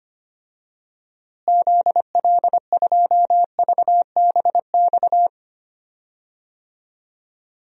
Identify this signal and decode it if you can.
Morse code, heard as a keyed tone, about 25 wpm, 700 Hz. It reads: ZL2VBX
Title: DXCC-100 Call Signs - 6 Characters - Repeated 2x - 25wpm